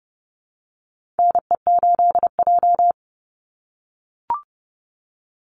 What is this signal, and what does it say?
Morse code, heard as a keyed tone, about 30 wpm, 700 Hz. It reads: NE8J
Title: Call Signs - 30wpm